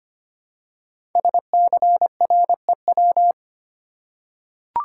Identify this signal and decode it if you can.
Morse code, heard as a keyed tone, about 25 wpm, 700 Hz. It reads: SCREW
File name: 5 Letter Words - First 3 Letters Given - 25wpm